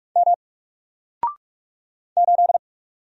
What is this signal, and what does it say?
Morse code, heard as a keyed tone, about 45 wpm, 700 Hz. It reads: M 8
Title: Single Letter-Number Alt 45wpm